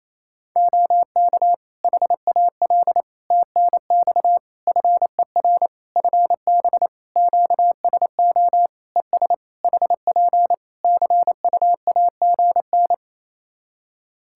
Morse code, heard as a keyed tone, about 28 wpm, 700 Hz. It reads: OK HAL TNX FER FB QSO ES HP CUAGN
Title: Practice QSO - Exchange 4 - Ending - 28wpm